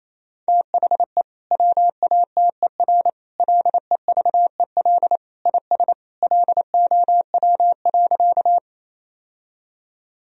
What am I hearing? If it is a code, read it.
Morse code, heard as a keyed tone, about 28 wpm, 700 Hz. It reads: THE WATER LEVEL IS LOW.
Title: Sentences from Top 1000 Words - Review Part 4 - 28wpm